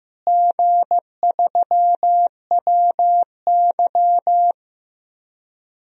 Morse code, heard as a keyed tone, about 15 wpm, 700 Hz. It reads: G3WY
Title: DXCC-100 Call Signs - 4 Characters - Repeated 3x - 15wpm